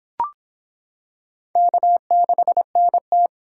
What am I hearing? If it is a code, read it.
Morse code, heard as a keyed tone, about 25 wpm, 700 Hz. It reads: K6NT